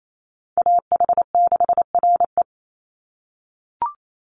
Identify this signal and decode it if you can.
Morse code, heard as a keyed tone, about 28 wpm, 700 Hz. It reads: AH6RE
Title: Call Signs - 28wpm